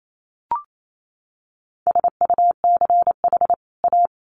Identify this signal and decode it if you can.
Morse code, heard as a keyed tone, about 28 wpm, 700 Hz. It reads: SUCH A
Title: Sets of 2 Words 28wpm